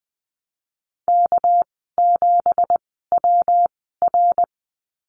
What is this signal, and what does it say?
Morse code, heard as a keyed tone, about 20 wpm, spaced as for 15 wpm, 700 Hz. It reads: K7WR
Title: 231 Lesson 37g - Letter x - Call Signs 20wpm